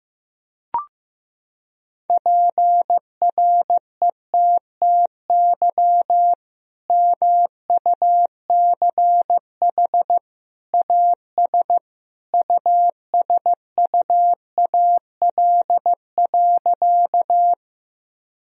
Morse code, heard as a keyed tone, about 14 wpm, 700 Hz. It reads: PRETTY MUCH AS USUAL.
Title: Sentences from Top 1000 Words - Review Part 2 - 15wpm